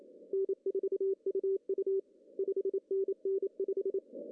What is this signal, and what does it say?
Morse code, heard as a keyed tone, about 27 wpm, 410 Hz. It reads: N4UU 5NN 5
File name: CW / cqww-cw-k5tr